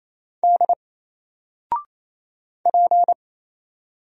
Morse code, heard as a keyed tone, about 27 wpm, 700 Hz. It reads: D P
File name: Single-Letter-Number-28wpm